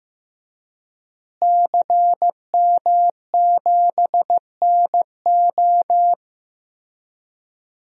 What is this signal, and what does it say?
Morse code, heard as a keyed tone, about 15 wpm, 700 Hz. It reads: CM7NO
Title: DXCC-100 Call Signs - 5 Characters - Repeated 3x - 15wpm